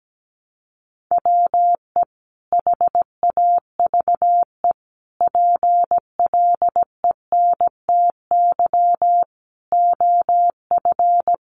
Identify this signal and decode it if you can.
Morse code, heard as a keyed tone, about 17 wpm, 700 Hz. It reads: WE HAVE PLENTY OF
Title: Sentences from Top 1300 Words 17wpm